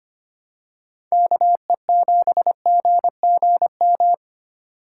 Morse code, heard as a keyed tone, about 29 wpm, 700 Hz. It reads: KE7GGM
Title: Speed Racing - USA Call Signs 25wpm